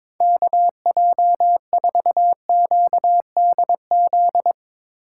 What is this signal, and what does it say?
Morse code, heard as a keyed tone, about 21 wpm, 700 Hz. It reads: KJ4QDZ
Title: USA Call Signs - 6 Characters - 4x Character Spacing - 22wpm